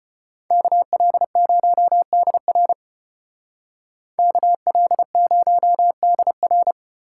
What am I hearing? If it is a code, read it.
Morse code, heard as a keyed tone, about 33 wpm, 700 Hz. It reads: KL0DR KL0DR